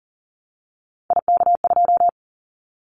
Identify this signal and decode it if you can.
Morse code, heard as a keyed tone, about 40 wpm, 700 Hz. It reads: IK2
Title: Callsign Prefixes for Top 25 DXCC - Repeated 3x - 40wpm